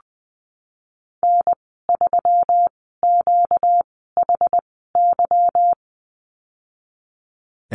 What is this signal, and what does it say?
Morse code, heard as a keyed tone, about 19 wpm, spaced as for 14 wpm, 700 Hz. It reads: N3QHY
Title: 238 Lesson 38g - Letter q - Call Signs 20wpm